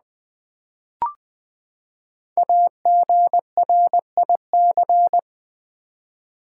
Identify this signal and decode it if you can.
Morse code, heard as a keyed tone, about 20 wpm, 700 Hz. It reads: AGRIC
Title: Build A Word - 5 Letters At A Time - 12-Letter Words - 20wpm